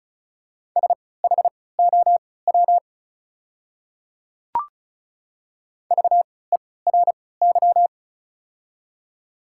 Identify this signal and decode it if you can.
Morse code, heard as a keyed tone, about 34 wpm, spaced as for 20 wpm, 690 Hz. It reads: SHOW VERY
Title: 3x Character Spacing - Top 200 Words - 35wpm